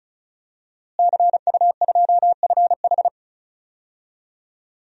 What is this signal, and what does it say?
Morse code, heard as a keyed tone, about 36 wpm, 700 Hz. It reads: CU2FH
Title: DXCC-100 Call Signs - 5 Characters - Repeated 3x - 35wpm